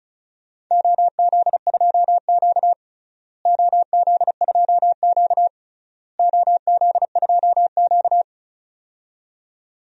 Morse code, heard as a keyed tone, about 34 wpm, 700 Hz. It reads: OZ2Q OZ2Q OZ2Q
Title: DXCC-100 Call Signs - 4 Characters - Repeated 3x - 35wpm